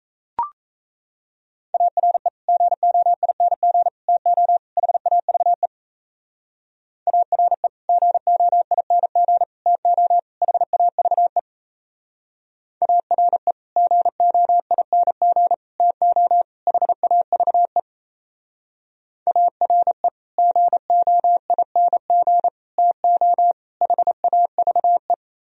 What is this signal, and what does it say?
Morse code, heard as a keyed tone, about 37 wpm, 700 Hz. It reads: ARE GOING TO HAVE ARE GOING TO HAVE ARE GOING TO HAVE ARE GOING TO HAVE